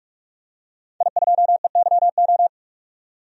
Morse code, heard as a keyed tone, about 46 wpm, 700 Hz. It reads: I2EYO
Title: DXCC-100 Call Signs - 5 Characters - 45wpm